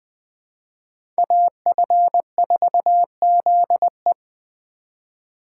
Morse code, heard as a keyed tone, about 21 wpm, 700 Hz. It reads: AF4ZE